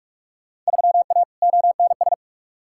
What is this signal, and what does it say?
Morse code, heard as a keyed tone, about 46 wpm, 700 Hz. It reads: 3A ONS
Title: Field Day Exchanges - 45wpm